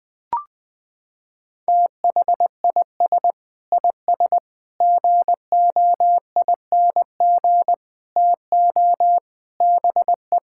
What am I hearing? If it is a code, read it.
Morse code, heard as a keyed tone, about 20 wpm, 700 Hz. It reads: THIS IS GOING TO BE
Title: Sentences with 5-Word Phrases - 20wpm